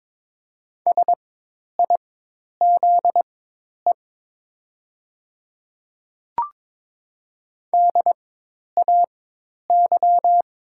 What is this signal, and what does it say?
Morse code, heard as a keyed tone, about 22 wpm, spaced as for 10 wpm, 700 Hz. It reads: SIZE DAY
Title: 4x Character Spacing - Top 500 Words - 22wpm